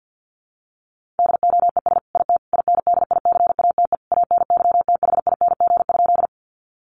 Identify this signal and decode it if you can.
Morse code, heard as a keyed tone, about 49 wpm, 700 Hz. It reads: DOES IT INDICATE ANYTHING?
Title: Sentences from Top 1300 Words 50wpm